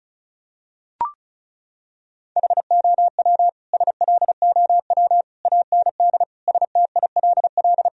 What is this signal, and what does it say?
Morse code, heard as a keyed tone, about 36 wpm, 700 Hz. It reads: HOW SLOW AND STILL
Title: Sentences from Top 1000 Words - Review Part 2 - 35wpm